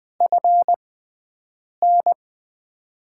Morse code, heard as a keyed tone, about 20 wpm, spaced as for 7 wpm, 700 Hz. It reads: FN
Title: USA Call Signs - 5 Characters - 6x Character Spacing - 20wpm